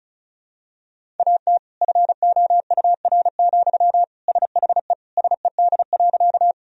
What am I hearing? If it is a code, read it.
Morse code, heard as a keyed tone, about 36 wpm, 700 Hz. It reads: AT FOUR, SHE SED.
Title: Sentences from Top 300 Words and Abbreviations 35wpm